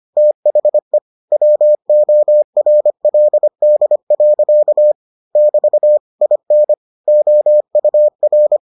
Morse code, heard as a keyed tone, about 25 wpm, 600 Hz. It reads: THE WORLD. <BT> IN OUR